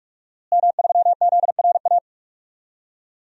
Morse code, heard as a keyed tone, about 48 wpm, 700 Hz. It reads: M3ZRA
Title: DXCC-100 Call Signs - 5 Characters - Repeated 2x - 45wpm